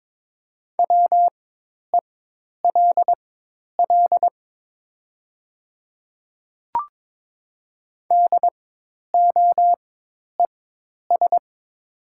Morse code, heard as a keyed tone, about 22 wpm, spaced as for 10 wpm, 700 Hz. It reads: WELL DOES
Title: 4x Character Spacing - Top 200 Words - 22wpm